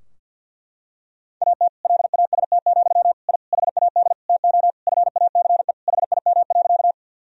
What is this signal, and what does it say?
Morse code, heard as a keyed tone, about 52 wpm, 700 Hz. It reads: AT LAST, I HAD TO FACE HIM.
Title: Sentences from Top 1000 Words - Review Part 2 - 50wpm